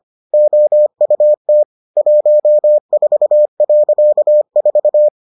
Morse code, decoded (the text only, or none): OUT 14.4